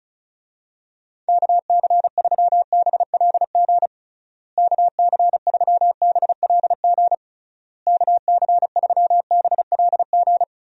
Morse code, heard as a keyed tone, about 36 wpm, 700 Hz. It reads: KC3BLG KC3BLG KC3BLG
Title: USA Call Signs - 6 Characters - Repeated 3x - 35wpm